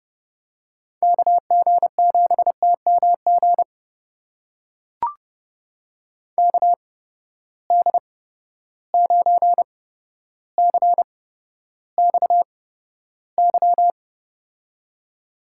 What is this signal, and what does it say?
Morse code, heard as a keyed tone, about 30 wpm, spaced as for 13 wpm, 700 Hz. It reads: KG7TMG KD9CXY
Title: USA Call Signs - 8x Character Spacing - 30wpm